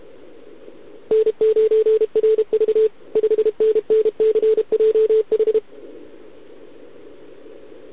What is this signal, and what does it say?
Morse code, heard as a keyed tone, about 32 wpm, 440 Hz. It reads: N9RV 5NNCJH
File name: ARCHIV / OKDX2020 / OK1RIb